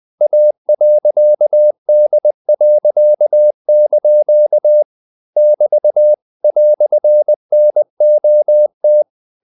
Morse code, heard as a keyed tone, about 20 wpm, 600 Hz. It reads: A.D.) <BT> "NOT
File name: CW / Practice / EnglishQuotations0320